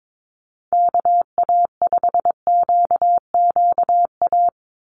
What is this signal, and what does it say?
Morse code, heard as a keyed tone, about 22 wpm, 700 Hz. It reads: KA5QQA
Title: NA SOTA Chaser Callsigns 22wpm